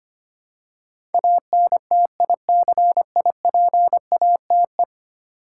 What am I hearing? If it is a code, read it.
Morse code, heard as a keyed tone, about 25 wpm, 700 Hz. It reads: ANTICIPATE
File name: Ten Letter Words - 25wpm